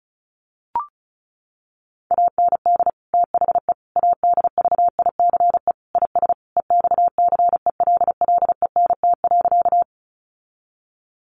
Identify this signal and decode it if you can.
Morse code, heard as a keyed tone, about 34 wpm, 700 Hz. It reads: AND THE ADVICE IS EXCELLENT.
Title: Sentences from Top 1100 Words 35wpm